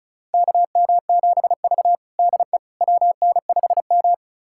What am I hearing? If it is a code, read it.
Morse code, heard as a keyed tone, about 36 wpm, 700 Hz. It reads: KM7V DE WN5M
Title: USA Callsign DE Callsign - 4 Characters - 35wpm